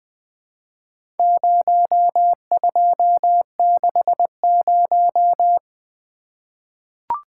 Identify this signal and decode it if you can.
Morse code, heard as a keyed tone, about 20 wpm, 700 Hz. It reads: 0260
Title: Four Letter-Number Combinations - 20wpm